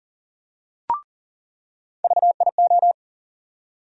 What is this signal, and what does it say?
Morse code, heard as a keyed tone, about 40 wpm, 700 Hz. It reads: VIO
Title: USA Call Sign Postfixes - 3 Letters 40wpm